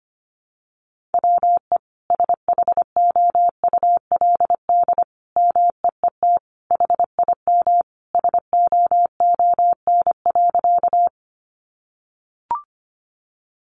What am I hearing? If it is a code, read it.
Morse code, heard as a keyed tone, about 25 wpm, 700 Hz. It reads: WE SHOULD MEET HIM SOON.
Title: Sentences from Top 1100 Words 25wpm